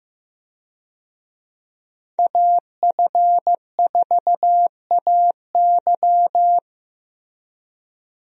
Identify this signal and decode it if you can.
Morse code, heard as a keyed tone, about 15 wpm, 700 Hz. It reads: AF4AY
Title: USA Call Signs - 5 Characters - Repeated 3x - 15wpm